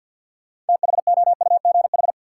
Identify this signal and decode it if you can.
Morse code, heard as a keyed tone, about 51 wpm, 700 Hz. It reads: THOUGH
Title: Sentences from Top 400 Words - New Word Drill - 50wpm